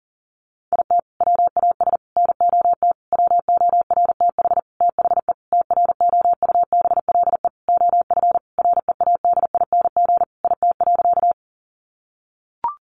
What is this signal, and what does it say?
Morse code, heard as a keyed tone, about 40 wpm, 700 Hz. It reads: IT WAS NOT WORTH THE TROUBLE OF READING IT.
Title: Sentences from Top 1200 Words 40wpm